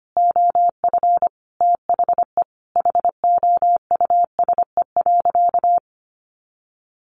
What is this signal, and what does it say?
Morse code, heard as a keyed tone, about 25 wpm, 700 Hz. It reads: OF THE HOUSE.
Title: Sentences from Top 1400 Words 25wpm